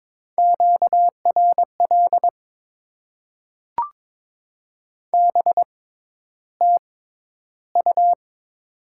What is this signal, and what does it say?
Morse code, heard as a keyed tone, about 22 wpm, spaced as for 11 wpm, 700 Hz. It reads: QRL BTU
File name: Top 20 QSO Elements - 6x Character Spacing - 22wpm